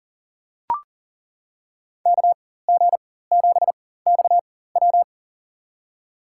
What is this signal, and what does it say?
Morse code, heard as a keyed tone, about 40 wpm, spaced as for 19 wpm, 700 Hz. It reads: KG7XW
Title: USA Call Signs - 5 Characters - 4x Character Spacing - 40wpm